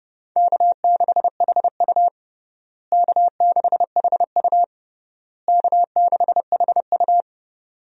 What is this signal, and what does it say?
Morse code, heard as a keyed tone, about 32 wpm, 700 Hz. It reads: K6HU K6HU K6HU
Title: USA Call Signs - 4 Characters - Repeated 3x - 30wpm